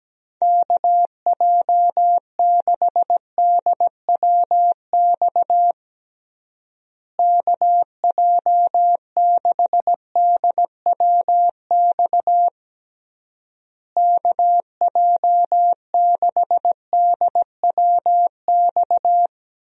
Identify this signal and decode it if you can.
Morse code, heard as a keyed tone, about 17 wpm, 700 Hz. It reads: KJ6DWX KJ6DWX KJ6DWX
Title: USA Call Signs - 6 Characters - Repeated 3x - 17wpm